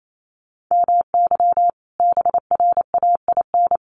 Morse code, heard as a keyed tone, about 28 wpm, 700 Hz. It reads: MY BRAIN
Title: Sentences from Top 1400 Words 28wpm